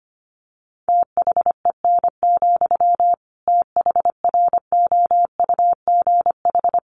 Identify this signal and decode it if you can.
Morse code, heard as a keyed tone, about 25 wpm, 700 Hz. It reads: THEN, THROUGH
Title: Sentences from Top 1400 Words 25wpm